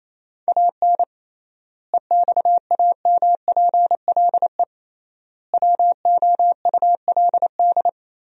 Morse code, heard as a keyed tone, about 28 wpm, 700 Hz. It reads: AN EXAMPLE WOULD
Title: 3x Word-Spacing - Sentences from Top 400 Words - 28wpm